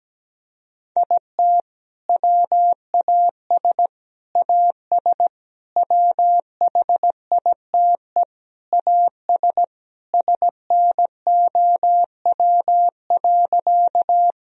Morse code, heard as a keyed tone, about 17 wpm, 700 Hz. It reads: IT WAS AS WHITE AS SNOW.
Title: Sentences from Top 1000 Words - Review Part 3 - 17wpm